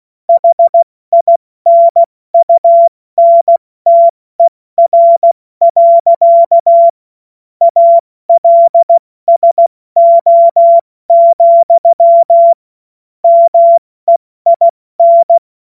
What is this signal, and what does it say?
Morse code, heard as a keyed tone, about 16 wpm, spaced as for 14 wpm, 680 Hz. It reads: HINUNTER. ALSO, MEIN
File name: Morsekurs / Witze / Witz 09 Januar 2024 70 F80